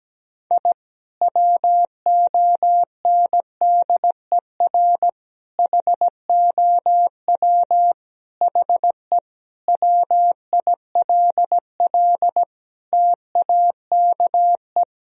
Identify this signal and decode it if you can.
Morse code, heard as a keyed tone, about 17 wpm, 700 Hz. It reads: I WONDER HOW HE WILL TAKE
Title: Sentences from Top 1000 Words - Review Part 3 - 17wpm